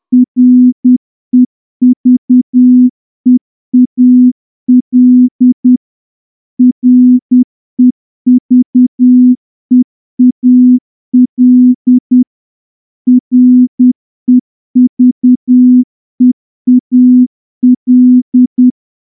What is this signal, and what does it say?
Morse code, heard as a keyed tone, about 9 wpm, 250 Hz. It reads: REVEAL REVEAL REVEAL
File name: morse-training-10WPM